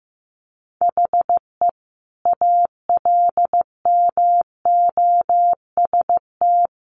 Morse code, heard as a keyed tone, about 15 wpm, 700 Hz. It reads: HE ALMOST
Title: Sentences from Top 700 Words 15wpm